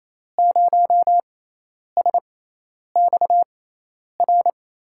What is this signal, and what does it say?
Morse code, heard as a keyed tone, about 27 wpm, spaced as for 10 wpm, 700 Hz. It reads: 0SXR
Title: USA Call Signs - 5 Characters - 6x Character Spacing - 28wpm